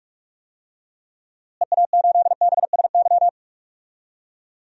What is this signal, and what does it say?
Morse code, heard as a keyed tone, about 46 wpm, 700 Hz. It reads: EA8BSY
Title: DXCC-100 Call Signs - 6 Characters - Repeated 2x - 45wpm